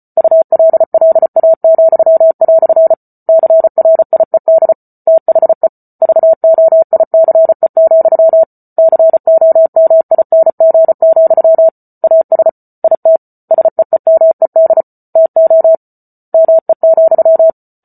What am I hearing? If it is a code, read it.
Morse code, heard as a keyed tone, about 34 wpm, 650 Hz. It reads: ULLA," CRIED THE VOICE, COMING, AS IT SEEMED TO ME,